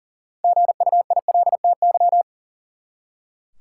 Morse code, heard as a keyed tone, about 40 wpm, 700 Hz. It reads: GUILTY
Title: Six Letter Words - 40wpm